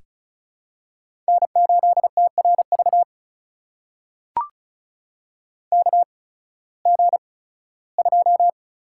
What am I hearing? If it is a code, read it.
Morse code, heard as a keyed tone, about 36 wpm, spaced as for 19 wpm, 700 Hz. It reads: N8TRV KG2
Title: USA Call Signs - 8x Character Spacing - 35wpm